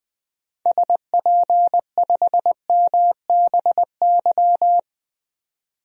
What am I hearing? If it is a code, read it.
Morse code, heard as a keyed tone, about 20 wpm, 700 Hz. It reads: SP5MBY
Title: DXCC-100 Call Signs - 6 Characters - 20wpm